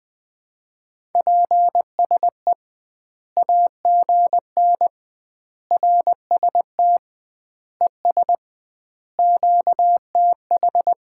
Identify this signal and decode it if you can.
Morse code, heard as a keyed tone, about 20 wpm, 700 Hz. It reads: PSE AGN RST ES QTH
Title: Common QSO Phrases - 2x Word-Spacing - 20wpm